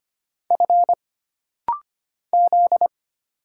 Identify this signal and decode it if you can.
Morse code, heard as a keyed tone, about 25 wpm, 700 Hz. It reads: F Z
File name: Single Character (Letter:Number:Punctuation:Prosign) - Rapid-Fire - 25wpm